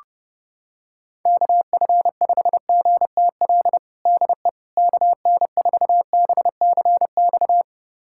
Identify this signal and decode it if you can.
Morse code, heard as a keyed tone, about 30 wpm, 700 Hz. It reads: KF5GTL DE KN4BCX
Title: USA Callsign DE Callsign 30wpm